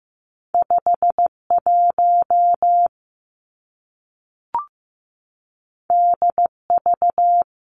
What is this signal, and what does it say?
Morse code, heard as a keyed tone, about 15 wpm, 700 Hz. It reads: 5 1 D V
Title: Two Challenging Letter-Number Combinations 15wpm